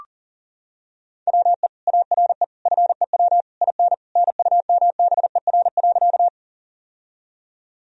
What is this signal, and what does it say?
Morse code, heard as a keyed tone, about 40 wpm, 700 Hz. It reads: WE ARE FEW IN NUMBER.
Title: Sentences from Top 1000 Words - Review Part 1 - 40wpm